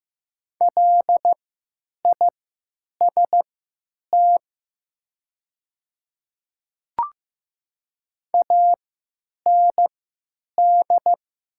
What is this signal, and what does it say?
Morse code, heard as a keyed tone, about 15 wpm, spaced as for 9 wpm, 700 Hz. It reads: LIST AND
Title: 3x Character Spacing - Top 500 Words - 15wpm